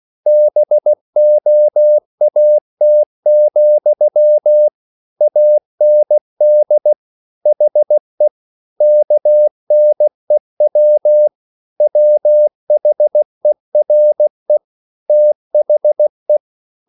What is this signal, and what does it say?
Morse code, heard as a keyed tone, about 16 wpm, 600 Hz. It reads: BOAT, AND HE KNEW WHERE THE